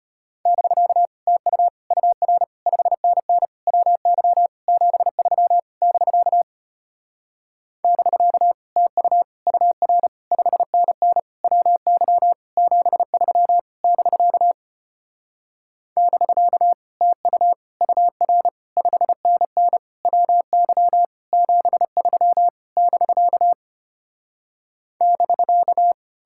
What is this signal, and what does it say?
Morse code, heard as a keyed tone, about 33 wpm, 700 Hz. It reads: <BK> TU UR 5NN WY 73 <BK> <BK> TU UR 5NN WY 73 <BK> <BK> TU UR 5NN WY 73 <BK> <BK>